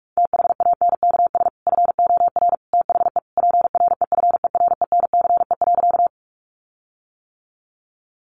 Morse code, heard as a keyed tone, about 44 wpm, 700 Hz. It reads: THANKS FOR THE PREFERENCE.
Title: Sentences from Top 1500 Words 45wpm